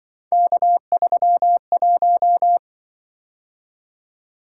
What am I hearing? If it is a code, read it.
Morse code, heard as a keyed tone, about 24 wpm, 700 Hz. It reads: K31
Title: Speed-Racing - Three Letter-Number Combinations - 20wpm